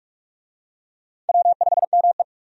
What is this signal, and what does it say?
Morse code, heard as a keyed tone, about 46 wpm, 700 Hz. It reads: W5ME
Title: USA Call Signs - 4 Characters - 45wpm